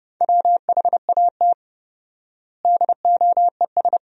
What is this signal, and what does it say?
Morse code, heard as a keyed tone, about 32 wpm, 700 Hz. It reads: WHAT DOES
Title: 4x Word-Spacing - Sentences from Top 400 Words - 30wpm